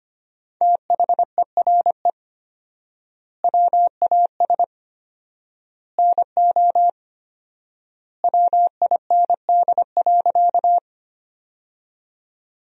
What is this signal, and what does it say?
Morse code, heard as a keyed tone, about 25 wpm, 700 Hz. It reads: THERE WAS NO WIND.
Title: 4x Word-Spacing - Sentences from Top 400 Words - 25wpm